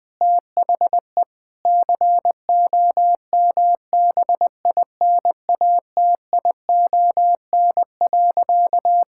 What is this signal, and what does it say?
Morse code, heard as a keyed tone, about 20 wpm, 700 Hz. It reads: THE COMBINATION.
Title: Sentences from Top 1000 Words - Review Part 6 - 20wpm